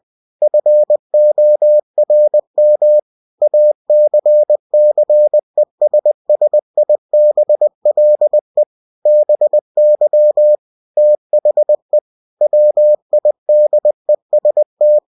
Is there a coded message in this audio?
Yes: Morse code from 0.4 to 15.0 s, heard as a keyed tone, about 20 wpm, 600 Hz. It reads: FORM ACCESSIBLE BY THE WIDEST